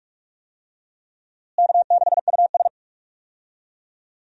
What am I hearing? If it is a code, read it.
Morse code, heard as a keyed tone, about 41 wpm, 700 Hz. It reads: K6US
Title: USA Call Signs - 4 Characters - Repeated 2x - 45wpm